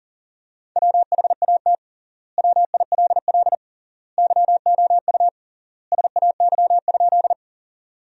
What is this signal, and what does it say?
Morse code, heard as a keyed tone, about 40 wpm, 700 Hz. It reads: WHAT WILL YOU SAY?
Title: 3x Word-Spacing - Sentences from Top 200 Words - 40wpm